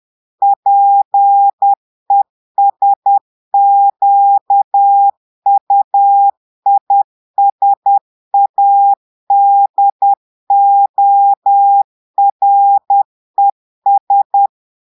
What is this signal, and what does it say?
Morse code, heard as a keyed tone, about 10 wpm, 800 Hz. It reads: PESQUISADORES
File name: Portugais - texte en clair 1 - vitesse - 10 mots minute